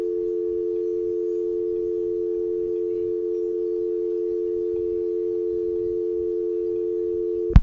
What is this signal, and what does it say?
Dial tone (North America), 350 + 440 Hz.
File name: Tape038 Side2 All Trey-Mark-Phreak-Out 2005-0912